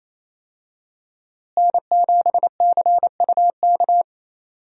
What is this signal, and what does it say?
Morse code, heard as a keyed tone, about 28 wpm, 700 Hz. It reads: N7CUK
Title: USA Call Signs - 5 Characters - Repeated 3x - 28wpm